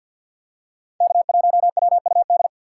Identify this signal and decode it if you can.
Morse code, heard as a keyed tone, about 49 wpm, 700 Hz. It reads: K1WUD
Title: USA Call Signs - 5 Characters - 4x Character Spacing - 50wpm